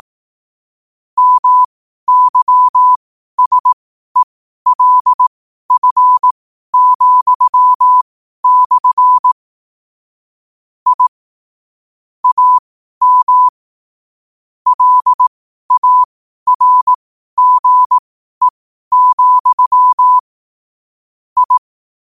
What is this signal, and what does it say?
Morse code, heard as a keyed tone, about 18 wpm, spaced as for 12 wpm, 1000 Hz. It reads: MYSELF,/ I AM LARGE, I
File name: qotdfiles / 12wpm